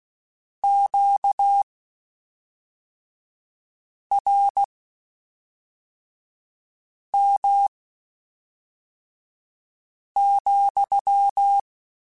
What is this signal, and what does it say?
Morse code, heard as a keyed tone, about 16 wpm, spaced as for 3 wpm, 800 Hz. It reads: QRM,
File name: Q Calls / Q Calls-mp3s / Q Calls SPSL2 @5